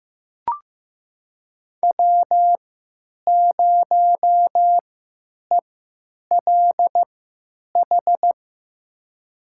Morse code, heard as a keyed tone, about 15 wpm, spaced as for 9 wpm, 700 Hz. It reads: W0ELH
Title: USA Call Signs - 5 Characters - 3x Character Spacing - 15wpm